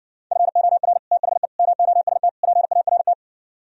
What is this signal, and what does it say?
Morse code, heard as a keyed tone, about 67 wpm, 700 Hz. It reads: FOR THE MOST PART